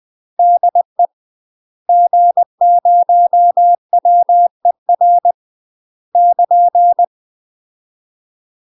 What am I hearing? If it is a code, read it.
Morse code, heard as a keyed tone, about 20 wpm, 700 Hz. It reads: DE G0WER <KN>
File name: UK MASTER QSOs TRAINING / 20WPM / UK MASTER QSOs 0010